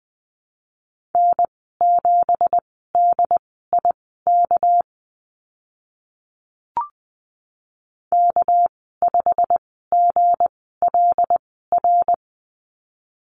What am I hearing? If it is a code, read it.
Morse code, heard as a keyed tone, about 20 wpm, spaced as for 14 wpm, 700 Hz. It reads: N7DIK K5GLR
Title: 213 Lesson 34g - Number 8 - Call Signs 20wpm